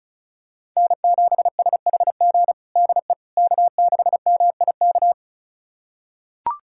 Morse code, heard as a keyed tone, about 34 wpm, 700 Hz. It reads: N7SHG DE K6MIK
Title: USA Callsign DE Callsign - 5 Characters - 35wpm